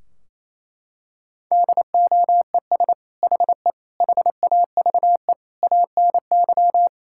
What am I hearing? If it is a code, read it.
Morse code, heard as a keyed tone, about 28 wpm, 700 Hz. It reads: DOES HE HAVE ANY
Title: Sentences from Top 1600 Words 28wpm